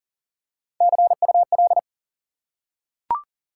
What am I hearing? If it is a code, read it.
Morse code, heard as a keyed tone, about 40 wpm, 700 Hz. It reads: CUL
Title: Top 90 QSO Elements 40wpm